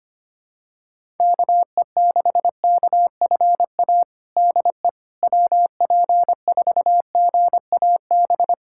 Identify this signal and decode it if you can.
Morse code, heard as a keyed tone, about 25 wpm, 700 Hz. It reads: KE6KFA DE WP4GAB
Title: USA Callsign DE Callsign - 6 Characters - 25wpm